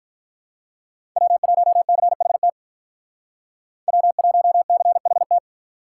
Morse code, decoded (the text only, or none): W1CHT W1CHT